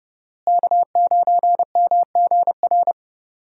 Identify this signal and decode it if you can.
Morse code, heard as a keyed tone, about 30 wpm, 700 Hz. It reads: K9MGR
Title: USA Call Signs - 5 Characters - Repeated 3x - 30wpm